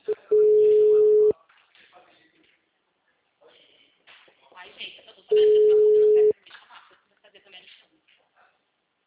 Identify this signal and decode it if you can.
Ringback tone (Europe, 425 Hz), with speech.
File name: out- -2026- - - .20350